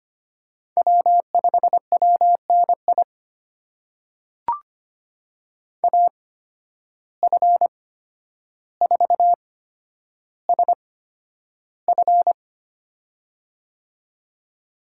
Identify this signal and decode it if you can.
Morse code, heard as a keyed tone, about 25 wpm, spaced as for 11 wpm, 700 Hz. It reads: W5WNI AF4SF
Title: USA Call Signs - 5 Characters - 8x Character Spacing - 25wpm